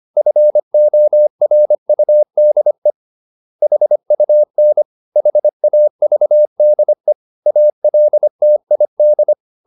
Morse code, heard as a keyed tone, about 25 wpm, 600 Hz. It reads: FORUDE HUN HAVDE ALTID
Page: Ferie 25 wpm